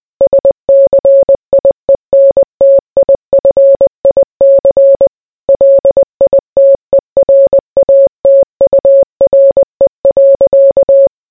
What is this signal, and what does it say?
Morse code, heard as a keyed tone, about 20 wpm, 550 Hz. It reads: SCIENTIFIC LITERATURE.